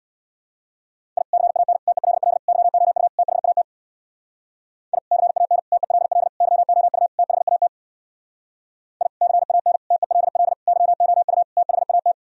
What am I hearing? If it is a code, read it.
Morse code, heard as a keyed tone, about 62 wpm, 700 Hz. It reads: I CAN TELL YOU THAT I CAN TELL YOU THAT I CAN TELL YOU THAT
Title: Speed-Racing - Sets of 5 Words 50wpm